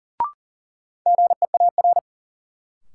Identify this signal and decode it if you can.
Morse code, heard as a keyed tone, about 40 wpm, 700 Hz. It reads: GEAR
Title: Sentences from Top 1400 Words - New Word Drill Alt - 40wpm